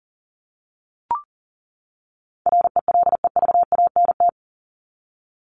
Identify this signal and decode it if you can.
Morse code, heard as a keyed tone, about 40 wpm, 700 Hz. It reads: RELEVANT
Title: Top 1001-1250 Words 40wpm